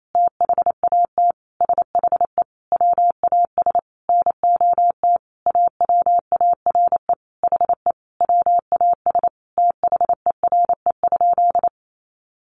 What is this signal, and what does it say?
Morse code, heard as a keyed tone, about 28 wpm, 700 Hz. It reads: THAT SHE WAS NOT AWARE HE WAS THERE?
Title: Sentences from Top 1200 Words 28wpm